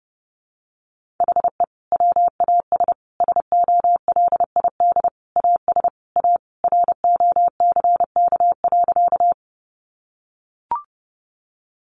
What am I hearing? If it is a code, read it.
Morse code, heard as a keyed tone, about 30 wpm, 700 Hz. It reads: HE WAS SOLID AS A ROCK.
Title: Sentences from Top 1500 Words 30wpm